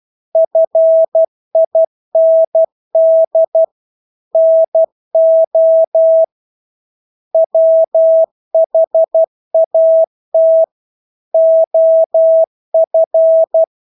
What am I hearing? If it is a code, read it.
Morse code, heard as a keyed tone, about 12 wpm, 650 Hz. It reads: FIND NO WHAT OF
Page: Engelske ord 12 wpm